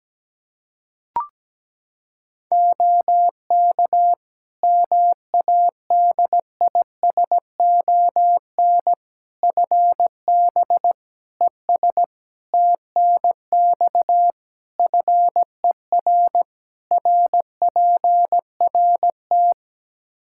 Morse code, heard as a keyed tone, about 17 wpm, 700 Hz. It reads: OK MADISON FB ES TNX FER RPRT
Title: Practice QSO - Exchange 2 - Info - 17wpm